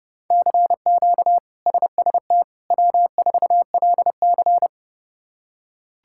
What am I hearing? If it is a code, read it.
Morse code, heard as a keyed tone, about 30 wpm, 700 Hz. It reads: CQ SST W4LC
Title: K1USN SST QSO Practice - 30wpm